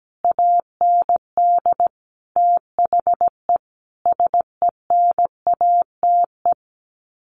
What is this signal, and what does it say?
Morse code, heard as a keyed tone, about 17 wpm, 700 Hz. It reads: AND THE SENATE
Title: Sets of 5 Words Encore2 17wpm